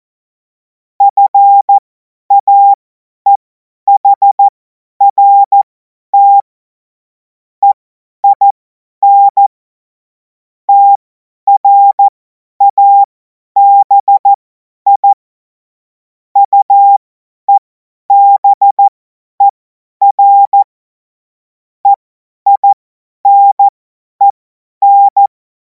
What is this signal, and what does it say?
Morse code, heard as a keyed tone, about 14 wpm, spaced as for 10 wpm, 800 Hz. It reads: FAEHRT EIN TRABI UEBER EINEN
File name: Morsekurs / Witze / Witz 30. Juni 2026 50 F70